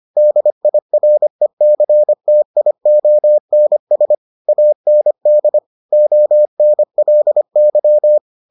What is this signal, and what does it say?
Morse code, heard as a keyed tone, about 25 wpm, 600 Hz. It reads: DIRECTIONS AND ONLY